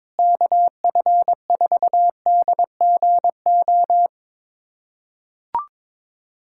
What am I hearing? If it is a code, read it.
Morse code, heard as a keyed tone, about 21 wpm, 700 Hz. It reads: KF4DGO
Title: USA Callsign DE Callsign - 6 Characters - 22wpm